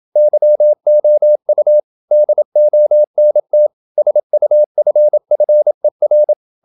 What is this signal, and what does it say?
Morse code, heard as a keyed tone, about 27 wpm, 600 Hz. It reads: YOU DONT SUFFER
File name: Never 0027wpm